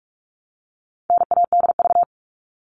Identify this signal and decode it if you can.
Morse code, heard as a keyed tone, about 44 wpm, 700 Hz. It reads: NADV
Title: All Cut Numbers - 4-Digit 45wpm